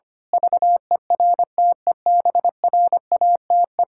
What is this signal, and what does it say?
Morse code, heard as a keyed tone, about 25 wpm, 700 Hz. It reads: VERTEBRATE
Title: Pangram Challenge v2 - 25wpm